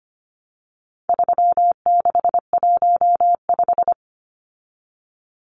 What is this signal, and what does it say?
Morse code, heard as a keyed tone, about 25 wpm, 700 Hz. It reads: 3615
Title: Numbers - 4-Digit 25wpm